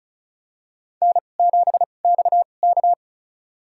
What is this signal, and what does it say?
Morse code, heard as a keyed tone, about 36 wpm, spaced as for 25 wpm, 700 Hz. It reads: N7XK
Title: USA Call Signs - 4 Characters - 2x Character Spacing - 35wpm